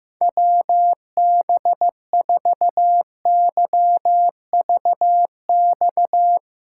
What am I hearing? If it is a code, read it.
Morse code, heard as a keyed tone, about 15 wpm, 700 Hz. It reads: WB4YVX
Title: USA Callsign DE Callsign - 6 Characters - 15wpm